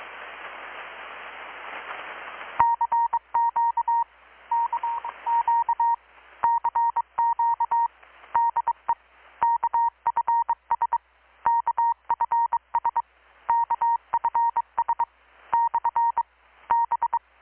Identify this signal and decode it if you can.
Morse code, heard as a keyed tone, about 24 wpm, 930 Hz. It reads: CQ KQ CQ DE KFS KFS KFS / B
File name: blog / swl / 6-abril-2019-12695-USB-KPS